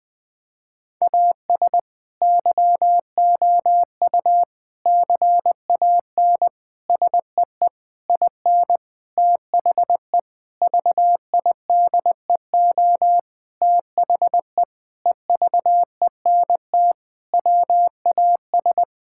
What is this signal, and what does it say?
Morse code, heard as a keyed tone, about 21 wpm, 700 Hz. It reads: AS YOU CAN SEE IN THE VIDEO THE EVENT WAS
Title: Sentences with 7-Word Phrases - 20wpm